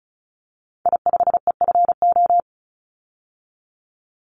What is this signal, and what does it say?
Morse code, heard as a keyed tone, about 36 wpm, 700 Hz. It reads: I5EFO
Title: Call Signs - 35wpm